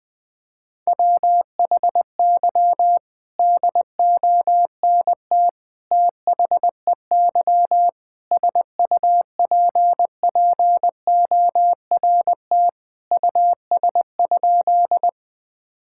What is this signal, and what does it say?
Morse code, heard as a keyed tone, about 20 wpm, 700 Hz. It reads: WHY DONT THEY SUPPORT US?
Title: Sentences from Top 1000 Words - Review Part 5 - 20wpm